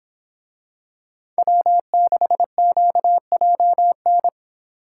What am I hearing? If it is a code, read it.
Morse code, heard as a keyed tone, about 26 wpm, 700 Hz. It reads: W6QJN
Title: Speed Racing - USA 1x3 Call Signs 22wpm